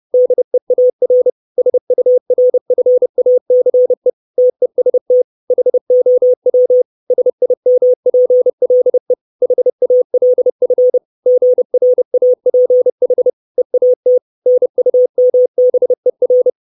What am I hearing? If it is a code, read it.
Morse code, heard as a keyed tone, about 30 wpm, 500 Hz. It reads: DEAR SURFACE TEST HOW SIMPLE HALF GRAPH EAT NUMBER
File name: 632 Words 30WPM